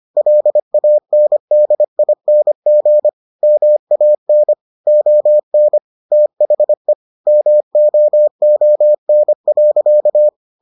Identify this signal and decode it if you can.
Morse code, heard as a keyed tone, about 25 wpm, 600 Hz. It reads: LANDING MAN ON THE MOON.